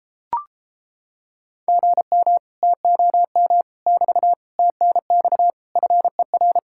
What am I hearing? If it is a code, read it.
Morse code, heard as a keyed tone, about 29 wpm, 700 Hz. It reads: GM TOM <BT> TNX FER